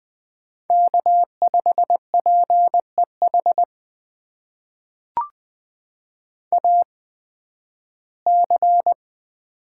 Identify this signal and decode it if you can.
Morse code, heard as a keyed tone, about 20 wpm, spaced as for 7 wpm, 700 Hz. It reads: K5PE HAC
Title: USA Call Signs - 8x Character Spacing - 20wpm